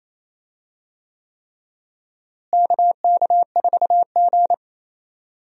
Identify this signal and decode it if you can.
Morse code, heard as a keyed tone, about 28 wpm, 700 Hz. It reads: KK4G
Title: USA Call Signs - 4 Characters - Repeated 3x - 28wpm